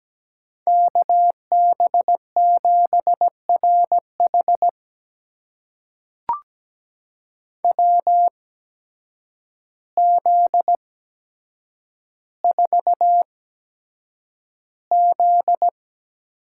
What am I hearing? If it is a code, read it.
Morse code, heard as a keyed tone, about 17 wpm, spaced as for 8 wpm, 700 Hz. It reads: KB7RH WZ4Z
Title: USA Call Signs - 5 Characters - 8x Character Spacing - 17wpm